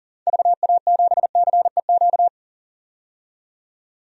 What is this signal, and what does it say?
Morse code, heard as a keyed tone, about 40 wpm, 700 Hz. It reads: VA7CEQ
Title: DXCC-100 Call Signs - 6 Characters - 40wpm